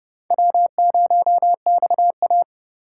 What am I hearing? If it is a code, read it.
Morse code, heard as a keyed tone, about 25 wpm, 690 Hz. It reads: W0XA